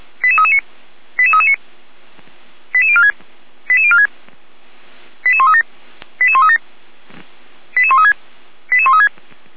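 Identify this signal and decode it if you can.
Five-tone selective call (ZVEI), heard as quick runs of five tones, tones of about 70 ms. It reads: 88339 88339 88046 88046 88137 88137 88136 88136
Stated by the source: ZVEI five-tone selective call received with an SDR, archived by the Orlamünde fire brigade (feuerwehr-orlamuende).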